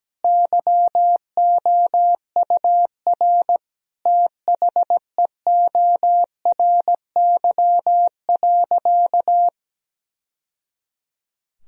Morse code, heard as a keyed tone, about 17 wpm, 700 Hz. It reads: YOUR THEORY.
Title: Sentences from Top 1000 Words - Review Part 3 - 17wpm